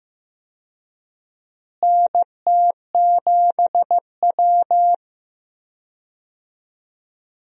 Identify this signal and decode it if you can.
Morse code, heard as a keyed tone, about 15 wpm, 700 Hz. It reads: NT7W
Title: USA Call Signs - 4 Characters - Repeated 2x - 15wpm